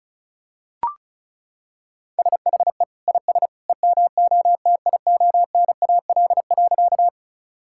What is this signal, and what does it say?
Morse code, heard as a keyed tone, about 34 wpm, 700 Hz. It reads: SHE IS EMOTIONAL.
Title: Sentences from Top 1000 Words - Review Part 5 - 35wpm